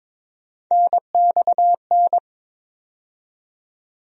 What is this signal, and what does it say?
Morse code, heard as a keyed tone, about 22 wpm, 700 Hz. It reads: NXN
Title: USA Call Sign Postfixes - 3 Letters 22wpm